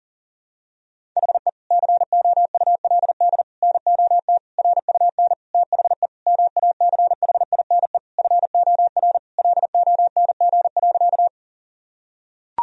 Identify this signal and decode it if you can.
Morse code, heard as a keyed tone, about 40 wpm, 700 Hz. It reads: HE COULD NOT RUN THE MACHINE FOR LONG.
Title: Sentences from Top 1000 Words - Review Part 2 - 40wpm